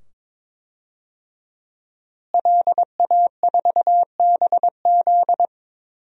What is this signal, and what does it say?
Morse code, heard as a keyed tone, about 22 wpm, 700 Hz. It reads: LA4BZ
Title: DXCC-100 Call Signs - 5 Characters - Repeated 2x - 22wpm